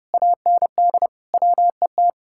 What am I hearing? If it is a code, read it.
Morse code, heard as a keyed tone, about 38 wpm, 700 Hz. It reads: AND WET